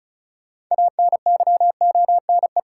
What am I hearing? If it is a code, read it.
Morse code, heard as a keyed tone, about 36 wpm, 700 Hz. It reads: ANYONE
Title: Sets of 5 Words Encore - New Word Drill - 35wpm